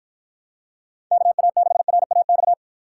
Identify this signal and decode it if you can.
Morse code, heard as a keyed tone, about 51 wpm, 700 Hz. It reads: KA6RAX